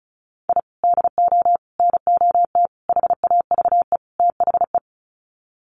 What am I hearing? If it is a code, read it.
Morse code, heard as a keyed tone, about 34 wpm, 700 Hz. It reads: I DO NOT HAVE THE
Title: Sets of 5 Words 35wpm